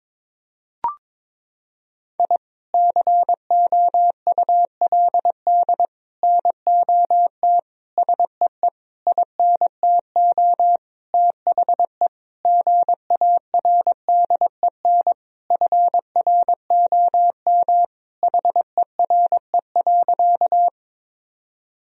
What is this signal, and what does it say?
Morse code, heard as a keyed tone, about 22 wpm, 700 Hz. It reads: I COULD NOT SEE INTO THE GARDEN FROM HERE.
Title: Sentences from Top 1000 Words - Review Part 4 - 22wpm